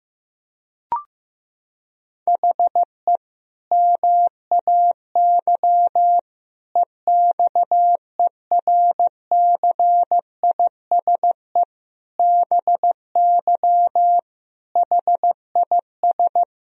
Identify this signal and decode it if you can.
Morse code, heard as a keyed tone, about 15 wpm, 700 Hz. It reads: HE MAY EXERCISE BY HIS
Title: Sentences from Top 1000 Words - Review Part 4 - 15wpm